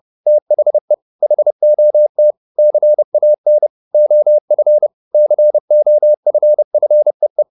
Morse code, heard as a keyed tone, about 30 wpm, 600 Hz. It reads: THE HOT CAN OF COFFEE